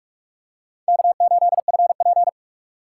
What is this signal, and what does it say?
Morse code, heard as a keyed tone, about 46 wpm, 700 Hz. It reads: K8FP
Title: USA Call Signs - 3x Character Spacing - 45wpm